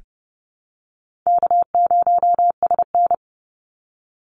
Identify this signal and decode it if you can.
Morse code, heard as a keyed tone, about 30 wpm, 700 Hz. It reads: K0SN
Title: Easy Call Signs Repeated 3x - 30wpm